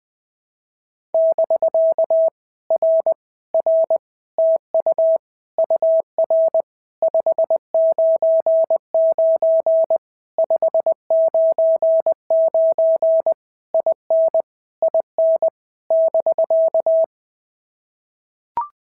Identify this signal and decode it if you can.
Morse code, heard as a keyed tone, about 20 wpm, 650 Hz. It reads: <BK> R R TU UR 599 599 IN IN <BK>
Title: POTA QSO Practice - 20wpm